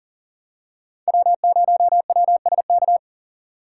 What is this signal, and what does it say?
Morse code, heard as a keyed tone, about 37 wpm, 700 Hz. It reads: W0WSK
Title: USA Call Signs - 5 Characters - 3x Character Spacing - 40wpm